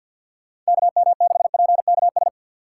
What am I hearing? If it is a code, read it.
Morse code, heard as a keyed tone, about 61 wpm, 700 Hz. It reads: KM6PKS
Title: Speed Racing - USA Call Signs 50wpm